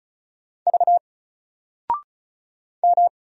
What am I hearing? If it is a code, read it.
Morse code, heard as a keyed tone, about 34 wpm, 700 Hz. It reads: V M
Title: Single Letter-Number 35wpm